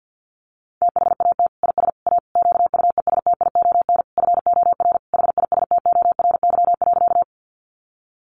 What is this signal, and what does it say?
Morse code, heard as a keyed tone, about 50 wpm, 700 Hz. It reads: THAT IS A QUESTION FOR HISTORY.
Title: Sentences from Top 600 Words 50wpm